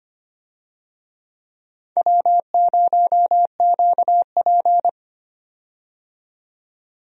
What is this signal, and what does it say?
Morse code, heard as a keyed tone, about 25 wpm, 700 Hz. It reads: W0QP
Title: USA Call Signs - 4 Characters - 25wpm